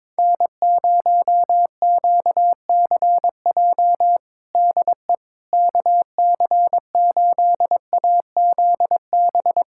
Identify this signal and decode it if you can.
Morse code, heard as a keyed tone, about 22 wpm, 700 Hz. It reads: N0QCJ DE KC8AZB
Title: Practice QSO - Exchange 4 - Ending - 22wpm